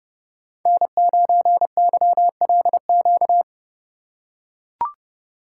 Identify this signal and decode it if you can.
Morse code, heard as a keyed tone, about 30 wpm, 700 Hz. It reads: N9YLQ
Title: USA Call Signs - 5 Characters - 6x Character Spacing - 30wpm